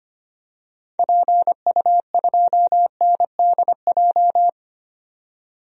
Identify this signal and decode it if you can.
Morse code, heard as a keyed tone, about 25 wpm, 700 Hz. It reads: PU2NDJ
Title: DXCC-100 Call Signs - 6 Characters - 25wpm